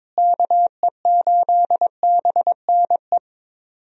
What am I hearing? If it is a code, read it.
Morse code, heard as a keyed tone, about 22 wpm, 700 Hz. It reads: KE8BNE
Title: USA Call Signs - 6 Characters - Repeated 2x - 22wpm